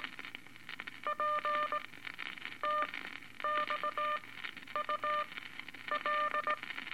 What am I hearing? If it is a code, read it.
Morse code, heard as a keyed tone, about 18 wpm, spaced as for 9 wpm, 1200 Hz, with noise. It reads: PTXUL
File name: Eight Words Per Minute